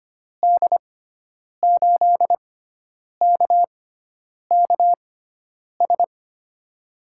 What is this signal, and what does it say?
Morse code, heard as a keyed tone, about 25 wpm, spaced as for 9 wpm, 700 Hz. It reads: D8KKS
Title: USA Call Signs - 6x Character Spacing - 25wpm